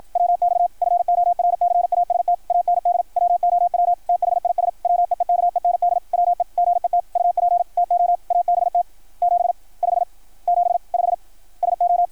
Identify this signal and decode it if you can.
Morse code, heard as a keyed tone, about 53 wpm, 660 Hz. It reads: WX FORCAST AND WOW THIS WEEKEND WE GET UP TO ABT 75 75 SO
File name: archived / CRSnet